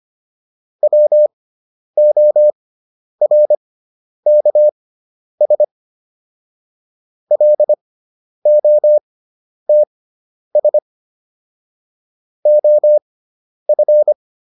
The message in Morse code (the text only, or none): WORKS LOTS OF